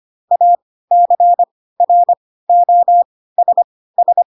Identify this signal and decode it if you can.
Morse code, heard as a keyed tone, about 25 wpm, spaced as for 16 wpm, 700 Hz. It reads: ACROSS
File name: Pinocchio / Pinocchio ch4 10 25 ik1hge